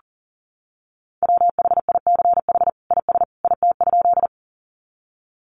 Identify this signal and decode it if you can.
Morse code, heard as a keyed tone, about 40 wpm, 700 Hz. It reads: WHICH IS IT?
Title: Sentences from Top 100 Words 40wpm